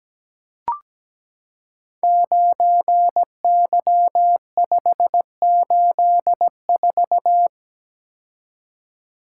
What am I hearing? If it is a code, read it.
Morse code, heard as a keyed tone, about 17 wpm, 700 Hz. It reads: 9Y584
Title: Code Groups - 5 Letter:Numbers - 17wpm